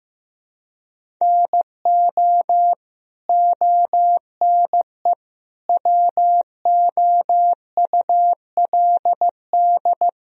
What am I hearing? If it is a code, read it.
Morse code, heard as a keyed tone, about 15 wpm, 700 Hz. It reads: NO ONE WOULD
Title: Sentences from Top 1000 Words - Review Part 1 - 15wpm